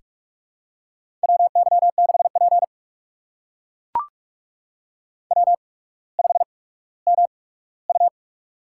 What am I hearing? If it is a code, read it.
Morse code, heard as a keyed tone, about 44 wpm, spaced as for 19 wpm, 700 Hz. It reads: WY6P W5MU
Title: USA Call Signs - 8x Character Spacing - 45wpm